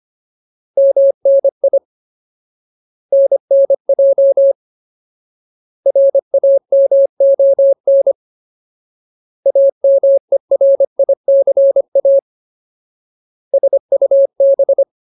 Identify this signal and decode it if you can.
Morse code, heard as a keyed tone, about 25 wpm, 550 Hz. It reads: MNI NNJ RAMON AMERICA SUB
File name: Ultimate Word List / 25WPM Ultimate